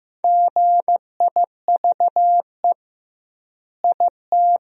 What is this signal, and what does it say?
Morse code, heard as a keyed tone, about 15 wpm, 700 Hz. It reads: GIVE IT
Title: 2x Word-Spacing - Sentences from Top 200 Words - 15wpm